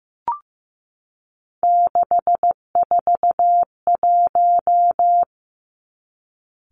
Morse code, heard as a keyed tone, about 15 wpm, 700 Hz. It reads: 641
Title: Three Challenging Letter-Number Combinations 15wpm